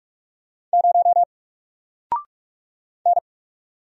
Morse code, heard as a keyed tone, about 44 wpm, 700 Hz. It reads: 0 N
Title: Single Letter-Number 45wpm